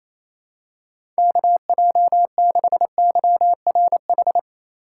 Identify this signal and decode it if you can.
Morse code, heard as a keyed tone, about 28 wpm, 700 Hz. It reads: KJ6YRH
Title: USA Call Signs - 6 Characters - 4x Character Spacing - 28wpm